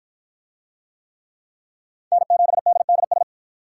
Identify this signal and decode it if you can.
Morse code, heard as a keyed tone, about 53 wpm, 700 Hz. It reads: N7DDS